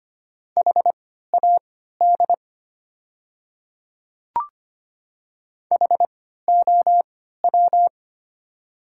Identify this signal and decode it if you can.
Morse code, heard as a keyed tone, about 25 wpm, spaced as for 14 wpm, 700 Hz. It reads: HAD HOW
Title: 3x Character Spacing - Top 100 Words - 25wpm